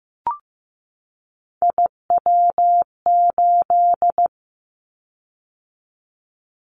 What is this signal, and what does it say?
Morse code, heard as a keyed tone, about 15 wpm, 700 Hz. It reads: IW8
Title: Callsign Prefixes for Top 50 DXCC - 15wpm